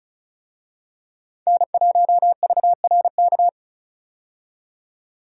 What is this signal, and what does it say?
Morse code, heard as a keyed tone, about 34 wpm, 700 Hz. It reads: N1VRK
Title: USA Call Signs - 5 Characters - 35wpm